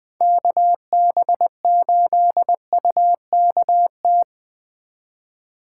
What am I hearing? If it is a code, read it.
Morse code, heard as a keyed tone, about 20 wpm, 700 Hz. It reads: KB8UKT
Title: Speed Racing - USA 2x3 Call Signs 17wpm